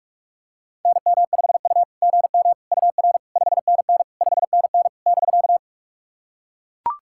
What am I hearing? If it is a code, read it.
Morse code, heard as a keyed tone, about 44 wpm, 660 Hz. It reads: NM5U GM UR 5NN 5NN <BK>
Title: SOTA QSO Practice 45wpm